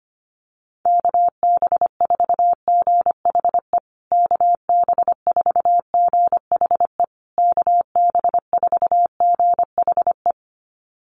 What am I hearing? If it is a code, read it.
Morse code, heard as a keyed tone, about 25 wpm, 700 Hz. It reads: KB4GHE KB4GHE KB4GHE
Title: USA Call Signs Repeated 3x 25wpm